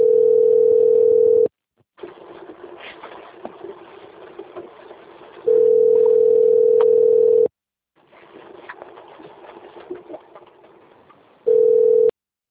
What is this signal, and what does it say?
Ringback tone (North America), 440 + 480 Hz.